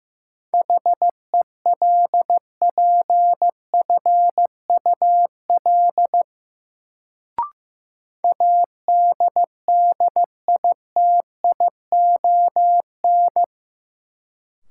Morse code, heard as a keyed tone, about 15 wpm, 700 Hz. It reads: HELPFUL ADDITION
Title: Sentences from Top 1000 Words - Review Part 6 - New Word Drill - 15wpm